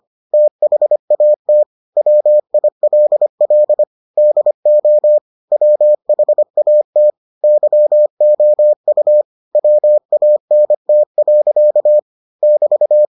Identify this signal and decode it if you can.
Morse code, heard as a keyed tone, about 25 wpm, 600 Hz. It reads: THAT WILL DO WHAT YOU WANT. <BT>